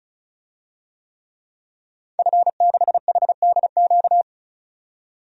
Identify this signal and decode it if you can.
Morse code, heard as a keyed tone, about 34 wpm, 700 Hz. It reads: F6HDQ
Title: DXCC-100 Call Signs - 5 Characters - 35wpm